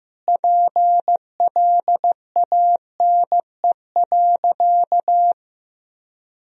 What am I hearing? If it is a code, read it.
Morse code, heard as a keyed tone, about 16 wpm, 700 Hz. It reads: PLANE.
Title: Sentences from Top 1000 Words - Review Part 3 - 15wpm